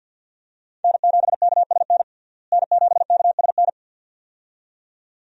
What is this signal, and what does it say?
Morse code, heard as a keyed tone, about 46 wpm, 700 Hz. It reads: N7KSN N7KSN
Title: USA Call Signs - 5 Characters - Repeated 2x - 50wpm